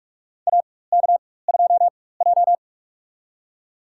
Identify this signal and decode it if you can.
Morse code, heard as a keyed tone, about 45 wpm, spaced as for 21 wpm, 700 Hz. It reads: AK2J
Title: USA Call Signs - 4 Characters - 4x Character Spacing - 45wpm